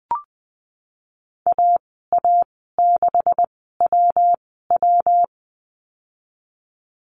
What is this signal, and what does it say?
Morse code, heard as a keyed tone, about 20 wpm, spaced as for 15 wpm, 700 Hz. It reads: AA6WW
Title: 147 Lesson 24g - Letter y - Call Signs 20wpm